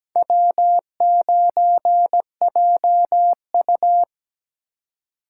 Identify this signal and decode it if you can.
Morse code, heard as a keyed tone, about 17 wpm, 700 Hz. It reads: W9JU
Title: USA Call Signs - 4 Characters - 3x Character Spacing - 17wpm